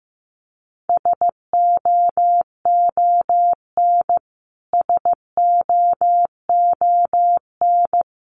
Morse code, heard as a keyed tone, about 15 wpm, 700 Hz. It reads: SOON SOON
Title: Top 500 Words Repeated 15wpm